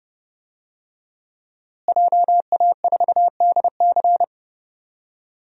Morse code, heard as a keyed tone, about 30 wpm, 700 Hz. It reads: JA4DC
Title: DXCC-100 Call Signs - 5 Characters - Repeated 2x - 30wpm